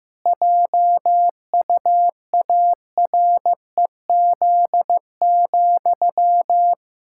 Morse code, heard as a keyed tone, about 14 wpm, 700 Hz. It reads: JUAREZ,
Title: Pangram Challenge v2 - 15wpm